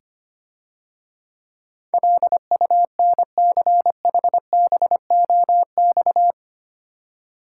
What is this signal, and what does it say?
Morse code, heard as a keyed tone, about 25 wpm, 700 Hz. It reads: LUNCHBOX
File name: Easy Words - Speed-Racing with Hints - 25wpm